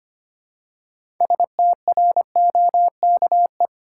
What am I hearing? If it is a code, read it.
Morse code, heard as a keyed tone, about 25 wpm, 700 Hz. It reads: STROKE
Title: Six Letter Words - 25wpm